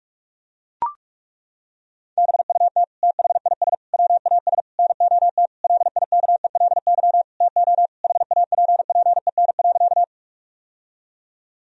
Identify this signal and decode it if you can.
Morse code, heard as a keyed tone, about 41 wpm, 700 Hz. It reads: BUT THIS WAS NOT LIKELY TO HAPPEN.
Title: Sentences from Top 1000 Words - Review Part 4 - 45wpm